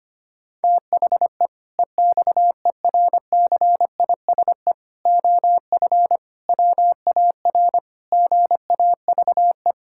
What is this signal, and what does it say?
Morse code, heard as a keyed tone, about 25 wpm, 700 Hz. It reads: THE EXERCISE OF WAR GAVE
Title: Sentences from Top 1000 Words - Review Part 6 - 25wpm